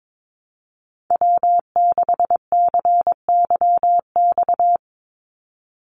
Morse code, heard as a keyed tone, about 22 wpm, 700 Hz. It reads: W6CYX
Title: Call Signs Repeated 3x - 22wpm